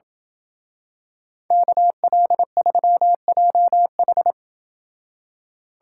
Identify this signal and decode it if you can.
Morse code, heard as a keyed tone, about 27 wpm, 700 Hz. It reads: KL3JH